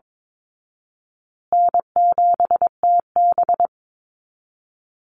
Morse code, heard as a keyed tone, about 22 wpm, 700 Hz. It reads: N7TB
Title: Call Signs Repeated 2x - 22wpm